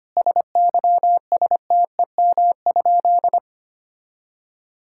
Morse code, heard as a keyed tone, about 25 wpm, 700 Hz. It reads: SYSTEM?
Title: Sentences from Top 1000 Words - Review Part 5 - 25wpm